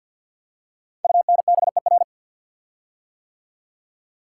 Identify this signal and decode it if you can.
Morse code, heard as a keyed tone, about 50 wpm, 690 Hz. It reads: UNBER
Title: Code Groups - 5 Letters - 50wpm